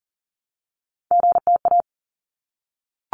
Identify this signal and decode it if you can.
Morse code, heard as a keyed tone, about 40 wpm, 700 Hz. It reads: GTA
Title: ARRL Sections 40wpm